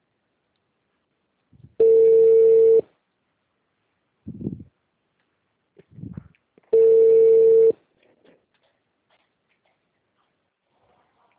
Ringback tone (China, 450 Hz).